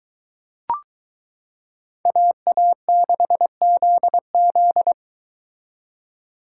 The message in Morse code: AA6ZZ

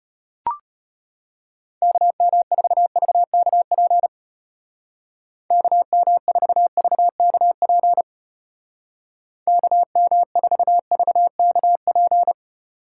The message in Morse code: KM4VKP KM4VKP KM4VKP

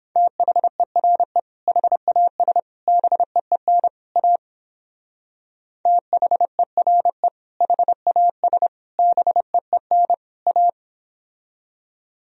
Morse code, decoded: THERE HAS BEEN A THERE HAS BEEN A

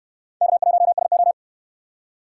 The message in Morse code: K1SO

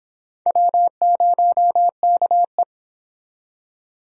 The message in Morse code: W0KE

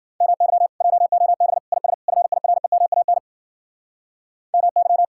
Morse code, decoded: MY JOB IS FIREMAN MY

